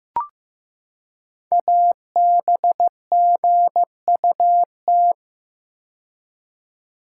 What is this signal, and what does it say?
Morse code, heard as a keyed tone, about 15 wpm, 690 Hz. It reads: ABGUT
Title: Code Groups - 5 Letters - 15wpm